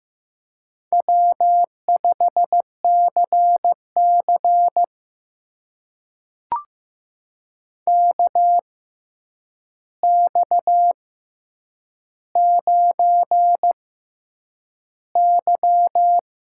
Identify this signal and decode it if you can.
Morse code, heard as a keyed tone, about 15 wpm, spaced as for 8 wpm, 700 Hz. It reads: W5CC KX9Y
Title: USA Call Signs - 4 Characters - 6x Character Spacing - 15wpm